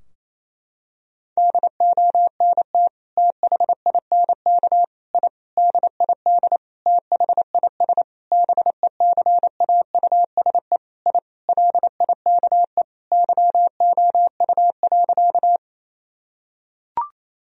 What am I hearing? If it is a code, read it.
Morse code, heard as a keyed tone, about 28 wpm, 700 Hz. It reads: DONT THINK I DID THIS BECAUSE I LIKE YOU.
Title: Sentences from Top 1000 Words - Review Part 3 - 28wpm